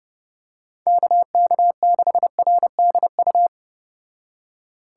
Morse code, heard as a keyed tone, about 32 wpm, 700 Hz. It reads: KK6RDU